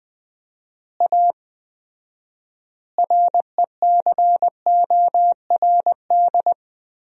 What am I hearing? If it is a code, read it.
Morse code, heard as a keyed tone, about 20 wpm, 700 Hz. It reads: A RECORD
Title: 4x Word-Spacing - Sentences from Top 500 Words - 20wpm